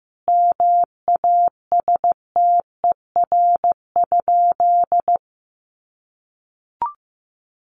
Morse code, heard as a keyed tone, about 15 wpm, 700 Hz. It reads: MASTER?
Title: Sentences from Top 1300 Words 15wpm